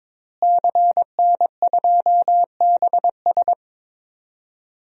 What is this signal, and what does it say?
Morse code, heard as a keyed tone, about 22 wpm, 700 Hz. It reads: CN2BS
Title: DXCC-100 Call Signs - 5 Characters - Repeated 2x - 22wpm